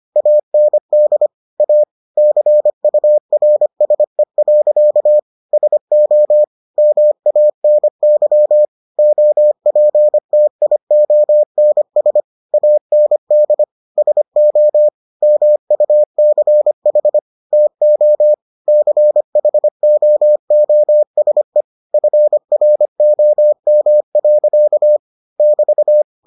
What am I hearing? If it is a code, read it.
Morse code, heard as a keyed tone, about 25 wpm, 600 Hz. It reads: AND A CURSE. SO MANY OPTIONS AND SO MUCH TO CHOOSE FROM. <BT>